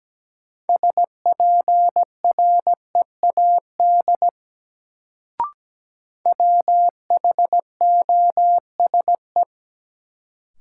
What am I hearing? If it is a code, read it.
Morse code, heard as a keyed tone, about 17 wpm, 700 Hz. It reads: SPREAD WHOSE
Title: Sentences from Top 1200 Words - New Word Drill Alt - 17wpm